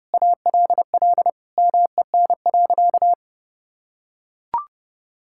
Morse code, heard as a keyed tone, about 30 wpm, 700 Hz. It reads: ALL MEN.
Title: Sentences from Top 1000 Words - Review Part 6 - 30wpm